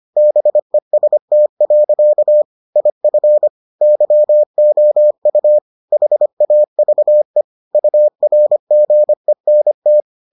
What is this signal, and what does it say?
Morse code, heard as a keyed tone, about 25 wpm, 600 Hz. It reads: BEST. IF YOU HAVE URGENT